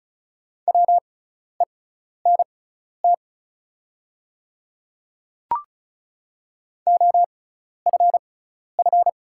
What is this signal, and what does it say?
Morse code, heard as a keyed tone, about 36 wpm, spaced as for 12 wpm, 690 Hz. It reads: WENT OFF
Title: 6x Character Spacing - Top 200 Words - 35wpm